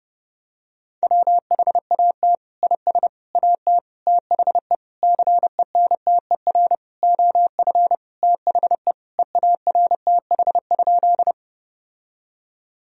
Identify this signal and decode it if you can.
Morse code, heard as a keyed tone, about 32 wpm, 700 Hz. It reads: WHAT IS AT THE CENTER OF THE EARTH?
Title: Sentences from Top 1000 Words - Review Part 2 - 30wpm